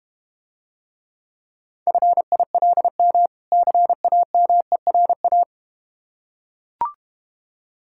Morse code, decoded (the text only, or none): FILM CAMERA